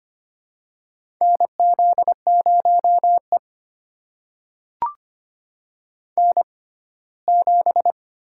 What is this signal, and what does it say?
Morse code, heard as a keyed tone, about 25 wpm, spaced as for 17 wpm, 700 Hz. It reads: NZ0E N7
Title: USA Call Signs - 4 Characters - 6x Character Spacing - 25wpm